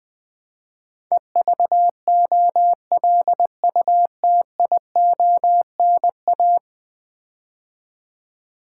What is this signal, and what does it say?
Morse code, heard as a keyed tone, about 20 wpm, 700 Hz. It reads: EVOLUTIONA
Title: Build A Word - 5 Letters At A Time - 12-Letter Words - 20wpm